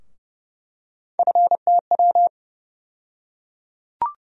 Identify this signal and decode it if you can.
Morse code, heard as a keyed tone, about 30 wpm, 700 Hz. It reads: FTW
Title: USA Call Sign Postfixes - 3 Letters 30wpm